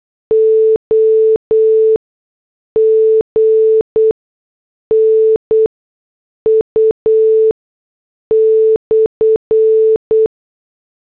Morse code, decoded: OGNU/